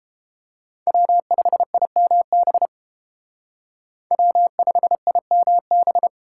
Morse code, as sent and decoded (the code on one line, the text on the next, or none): .-- ..... .. -- -... / .-- ..... .. -- -...
W5IMB W5IMB